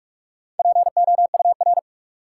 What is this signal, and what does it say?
Morse code, heard as a keyed tone, about 46 wpm, 700 Hz. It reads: POUR
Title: Four Letter Words - 45wpm